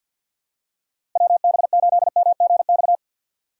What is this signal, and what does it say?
Morse code, heard as a keyed tone, about 52 wpm, 700 Hz. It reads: WB8MGX
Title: USA Call Signs - 4x Character Spacing - 50wpm